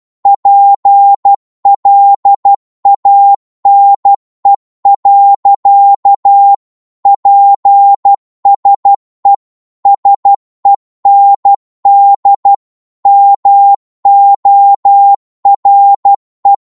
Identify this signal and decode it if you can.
Morse code, heard as a keyed tone, about 12 wpm, 800 Hz. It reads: PLANE. PSE SEND MORE
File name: Morse / qso-12wpm